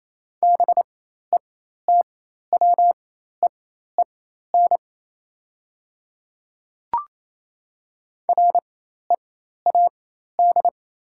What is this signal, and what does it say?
Morse code, heard as a keyed tone, about 28 wpm, spaced as for 13 wpm, 700 Hz. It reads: BETWEEN READ
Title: 4x Character Spacing - Top 500 Words - 28wpm